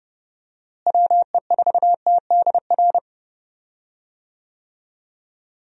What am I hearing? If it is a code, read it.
Morse code, heard as a keyed tone, about 31 wpm, 700 Hz. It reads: WE4TDR
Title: USA Call Signs - 6 Characters - Repeated 3x - 30wpm